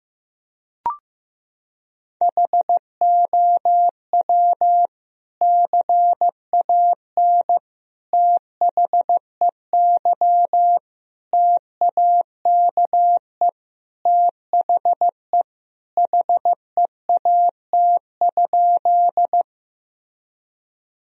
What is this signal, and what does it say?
Morse code, heard as a keyed tone, about 15 wpm, 700 Hz. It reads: HOW CAN THEY TAKE THE HEAT?
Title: Sentences from Top 1000 Words - Review Part 3 - 15wpm